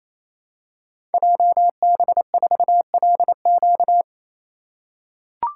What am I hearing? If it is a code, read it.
Morse code, heard as a keyed tone, about 27 wpm, 700 Hz. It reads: JB4LQ
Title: Code Groups - 5 Letter:Numbers - 28wpm